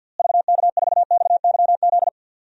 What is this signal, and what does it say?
Morse code, heard as a keyed tone, about 53 wpm, 700 Hz. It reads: VK4XYZ
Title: DXCC-100 Call Signs - 6 Characters - Repeated 3x - 50wpm